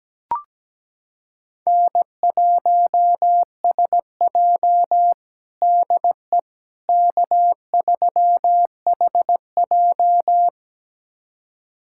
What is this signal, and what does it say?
Morse code, heard as a keyed tone, about 17 wpm, 700 Hz. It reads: N1SJ DE K3HJ
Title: USA Callsign DE Callsign - 4 Characters - 17wpm